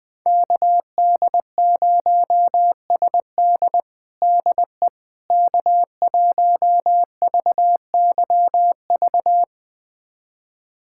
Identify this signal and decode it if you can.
Morse code, heard as a keyed tone, about 20 wpm, 690 Hz. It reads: KD0SD DE K1VYV
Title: USA Callsign DE Callsign 20wpm